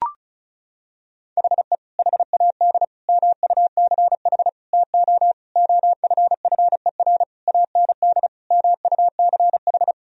Morse code, heard as a keyed tone, about 34 wpm, 700 Hz. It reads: HE HAD MUCH TO OFFER AND MUCH
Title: Sentences from Top 1000 Words - Review Part 4 - 35wpm